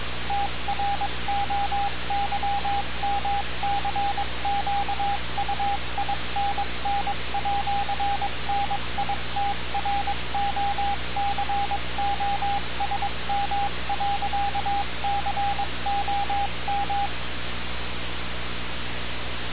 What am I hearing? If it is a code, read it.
Morse code, heard as a keyed tone, about 22 wpm, 800 Hz, with radio noise. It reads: TROYMCQUINN@NITROCOSM.COM
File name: radiocap / REC z